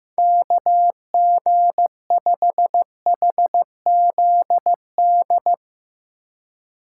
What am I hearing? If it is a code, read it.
Morse code, heard as a keyed tone, about 15 wpm, 700 Hz. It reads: KG5HZD
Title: USA Call Signs - 4x Character Spacing - 15wpm